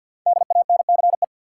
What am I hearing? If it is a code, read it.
Morse code, heard as a keyed tone, about 53 wpm, 700 Hz. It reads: DANCE
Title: Sentences from Top 700 Words - New Word Drill - 50wpm